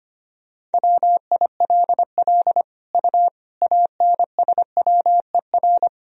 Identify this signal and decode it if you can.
Morse code, heard as a keyed tone, about 25 wpm, 700 Hz. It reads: WILL U ANSWER
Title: Sentences from Top 400 Words and Abbreviations 25wpm